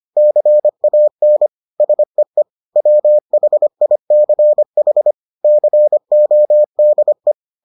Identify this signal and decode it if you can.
Morse code, heard as a keyed tone, about 25 wpm, 600 Hz. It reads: CAN SEE WHICH CODE